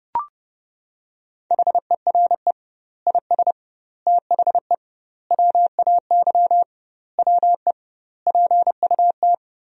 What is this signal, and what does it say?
Morse code, heard as a keyed tone, about 29 wpm, 700 Hz. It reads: HERE IS THE WAY WE PUT
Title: 2x Word-Spacing - Sentences from Top 200 Words - 30wpm